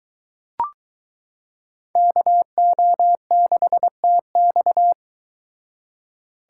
Morse code, heard as a keyed tone, about 23 wpm, 700 Hz. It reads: KO6TX